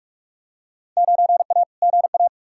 Morse code, heard as a keyed tone, about 46 wpm, 700 Hz. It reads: 9A GA
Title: Field Day Exchanges Encore - 45wpm